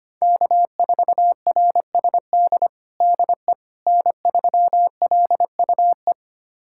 Morse code, heard as a keyed tone, about 25 wpm, 700 Hz. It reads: K4RSD DE N3LUE
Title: USA Callsign DE Callsign - 5 Characters - 25wpm